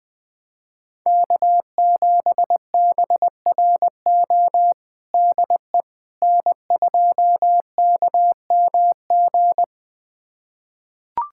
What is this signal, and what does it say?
Morse code, heard as a keyed tone, about 20 wpm, 700 Hz. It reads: K7BRO DE N2KMG
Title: USA Callsign DE Callsign - 5 Characters - 20wpm